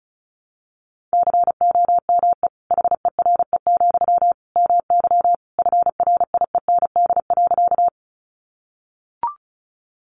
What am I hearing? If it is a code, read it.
Morse code, heard as a keyed tone, about 34 wpm, 700 Hz. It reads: COME HERE, MY FRIEND.
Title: Sentences from Top 300 Words 35wpm